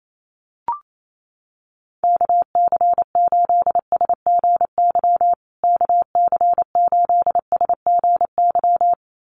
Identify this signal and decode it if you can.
Morse code, heard as a keyed tone, about 28 wpm, 700 Hz. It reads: KC8SGY KC8SGY
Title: USA Call Signs Repeated 2x 28wpm